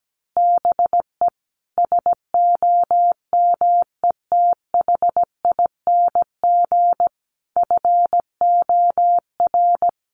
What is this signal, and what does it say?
Morse code, heard as a keyed tone, about 17 wpm, 700 Hz. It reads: BE SOMETHING FOR
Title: Sentences from Top 1200 Words 17wpm